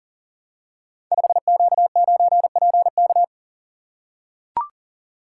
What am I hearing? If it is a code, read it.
Morse code, heard as a keyed tone, about 41 wpm, 700 Hz. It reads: 5Q9PK
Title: Code Groups - 5 Letter:Numbers - 40wpm